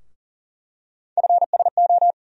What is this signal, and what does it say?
Morse code, heard as a keyed tone, about 40 wpm, 700 Hz. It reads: FSO
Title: USA Call Sign Postfixes - 3 Letters 40wpm